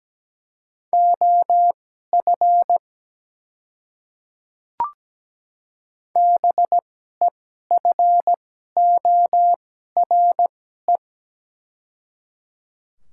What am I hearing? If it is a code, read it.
Morse code, heard as a keyed tone, about 17 wpm, spaced as for 12 wpm, 700 Hz. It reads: OF BEFORE
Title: 2x Character Spacing - Top 200 Words - 17wpm